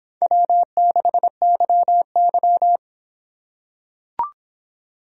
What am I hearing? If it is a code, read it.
Morse code, heard as a keyed tone, about 26 wpm, 700 Hz. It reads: W6YY